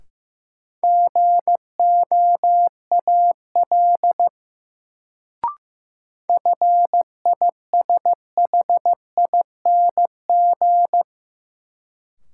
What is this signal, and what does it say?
Morse code, heard as a keyed tone, about 15 wpm, 700 Hz. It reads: GOAL FISHING
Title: Sentences from Top 1000 Words - Review Part 5 - New Word Drill - 15wpm